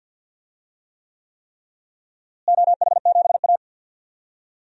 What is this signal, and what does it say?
Morse code, heard as a keyed tone, about 52 wpm, 700 Hz. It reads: OH7A
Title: DXCC-100 Call Signs - 4 Characters - Repeated 3x - 50wpm